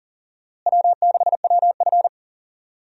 Morse code, heard as a keyed tone, about 40 wpm, 700 Hz. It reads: W6WF
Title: USA Call Signs - 4 Characters - Repeated 2x - 40wpm